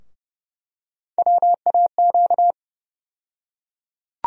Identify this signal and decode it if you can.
Morse code, heard as a keyed tone, about 30 wpm, 700 Hz. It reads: WAQ
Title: USA Call Sign Postfixes - 3 Letters 30wpm